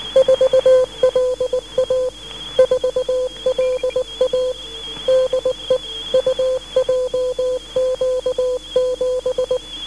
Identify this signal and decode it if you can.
Morse code, heard as a keyed tone, about 19 wpm, 510 Hz. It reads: 4LA 4LA DE UJQ7
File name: UJQ7 17130.5 0809